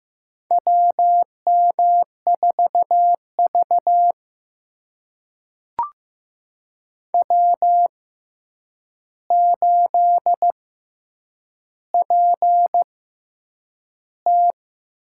Morse code, heard as a keyed tone, about 15 wpm, spaced as for 8 wpm, 700 Hz. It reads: WM4V W8PT
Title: USA Call Signs - 4 Characters - 6x Character Spacing - 15wpm